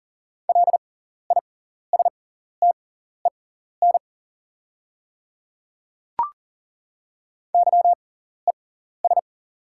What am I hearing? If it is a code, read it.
Morse code, heard as a keyed tone, about 37 wpm, spaced as for 14 wpm, 690 Hz. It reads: LISTEN YES
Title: 6x Character Spacing - Top 500 Words - 40wpm